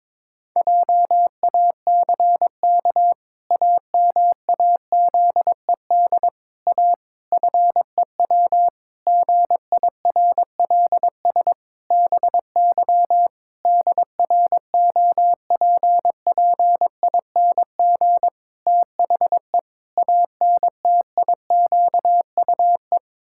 Morse code, heard as a keyed tone, about 21 wpm, 700 Hz. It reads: JACK AMAZED A FEW GIRLS BY DROPPING THE ANTIQUE
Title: Pangram Challenge v2 - 22wpm